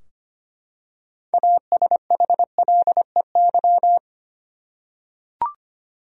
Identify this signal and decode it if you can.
Morse code, heard as a keyed tone, about 25 wpm, 700 Hz. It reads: ASHLEY
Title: USA Common Names - 25wpm